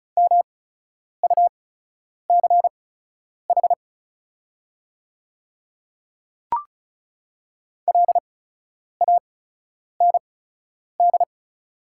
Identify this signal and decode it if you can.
Morse code, heard as a keyed tone, about 36 wpm, spaced as for 10 wpm, 700 Hz. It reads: MUCH LAND
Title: 8x Character Spacing - Top 300 Words - 35wpm